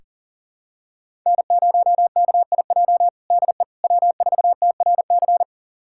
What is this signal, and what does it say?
Morse code, heard as a keyed tone, about 40 wpm, 700 Hz. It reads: N0KIJ DE W4TRC
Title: USA Callsign DE Callsign - 5 Characters - 40wpm